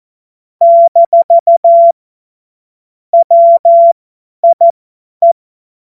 Morse code, heard as a keyed tone, about 14 wpm, spaced as for 10 wpm, 680 Hz. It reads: - WIE
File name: Morsekurs / Witze / Witz 22. August 2023 50 F70